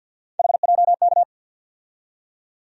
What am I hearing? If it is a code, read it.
Morse code, heard as a keyed tone, about 48 wpm, 700 Hz. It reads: HJK
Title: USA Call Sign Postfixes - 3 Letters 50wpm